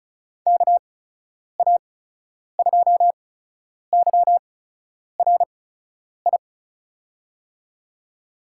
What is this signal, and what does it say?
Morse code, heard as a keyed tone, about 36 wpm, spaced as for 10 wpm, 700 Hz. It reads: KA2YRI
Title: USA Call Signs - 6 Characters - 8x Character Spacing - 35wpm